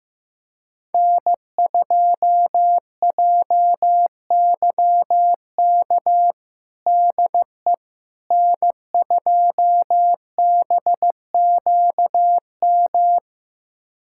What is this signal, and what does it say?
Morse code, heard as a keyed tone, about 15 wpm, 700 Hz. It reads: N2JYK DE N2BQM
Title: USA Callsign DE Callsign - 5 Characters - 15wpm